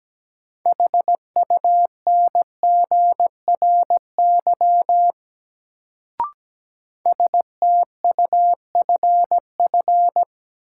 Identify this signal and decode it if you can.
Morse code, heard as a keyed tone, about 17 wpm, 700 Hz. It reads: HUNGRY STUFF
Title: Sentences from Top 1400 Words - New Word Drill - 17wpm